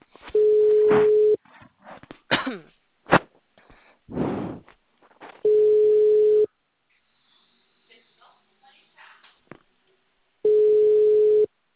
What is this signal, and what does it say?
Ringback tone (Europe, 425 Hz), with speech.
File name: out- -2001- - - .63661